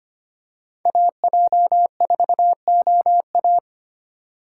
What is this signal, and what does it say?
Morse code, heard as a keyed tone, about 25 wpm, 690 Hz. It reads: AJ4OA
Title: USA Call Signs - 4x Character Spacing - 25wpm